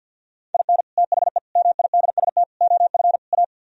Morse code, heard as a keyed tone, about 64 wpm, 700 Hz. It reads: IN THE MIDST OF A